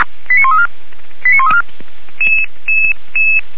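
Five-tone selective call (ZVEI), heard as quick runs of five tones, tones of about 70 ms. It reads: 89134 89134
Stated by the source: Five-tone ZVEI selective call received on a software-defined radio.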